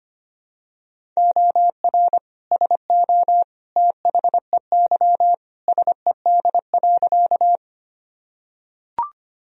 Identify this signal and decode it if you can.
Morse code, heard as a keyed tone, about 25 wpm, 700 Hz. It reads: OR SO THEY SED.
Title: Sentences from Top 100 Words and Abbreviations 25wpm